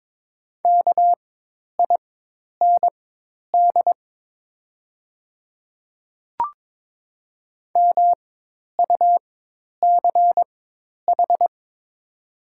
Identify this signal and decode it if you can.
Morse code, heard as a keyed tone, about 22 wpm, spaced as for 10 wpm, 700 Hz. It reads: KIND MUCH
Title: 4x Character Spacing - Top 200 Words - 22wpm